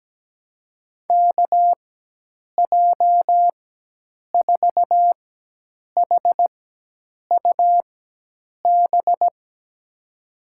Morse code, heard as a keyed tone, about 17 wpm, spaced as for 8 wpm, 700 Hz. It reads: KJ4HUB
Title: USA Call Signs - 4x Character Spacing - 17wpm